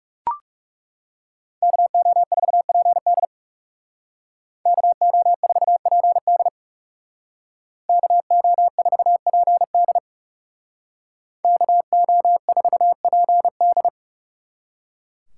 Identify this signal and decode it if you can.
Morse code, heard as a keyed tone, about 36 wpm, 700 Hz. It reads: KO4PD KO4PD KO4PD KO4PD
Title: Speed Racing - USA 2x2 Call Signs 30wpm